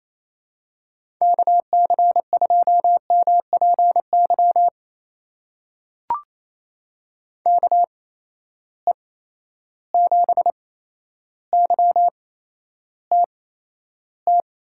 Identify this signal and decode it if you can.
Morse code, heard as a keyed tone, about 28 wpm, spaced as for 12 wpm, 700 Hz. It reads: KC2MPY KE7YTT
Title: USA Call Signs - 6 Characters - 8x Character Spacing - 28wpm